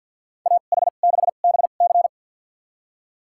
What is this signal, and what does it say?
Morse code, heard as a keyed tone, about 47 wpm, spaced as for 35 wpm, 700 Hz. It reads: AH6BC
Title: USA Call Signs - 2x Character Spacing - 50wpm